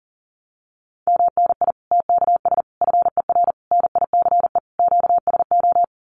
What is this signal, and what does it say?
Morse code, heard as a keyed tone, about 40 wpm, 700 Hz. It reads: MNI TKS FER NICE QSO
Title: Common QSO Phrases 40wpm